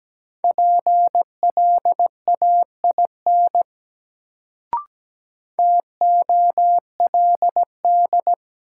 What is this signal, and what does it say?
Morse code, heard as a keyed tone, about 17 wpm, 700 Hz. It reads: PLAIN TOLD
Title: Sentences from Top 400 Words - New Word Drill - 17wpm